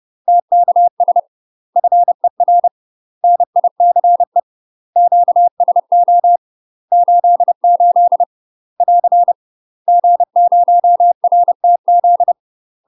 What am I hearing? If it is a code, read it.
Morse code, heard as a keyed tone, about 30 wpm, 700 Hz. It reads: TKS FER NICE QSO 88 <AR> G0RTZ
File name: UK MASTER QSOs TRAINING / 30WPM / UK MASTER QSOs 0004